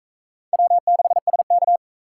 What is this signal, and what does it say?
Morse code, heard as a keyed tone, about 40 wpm, 700 Hz. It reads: W6SK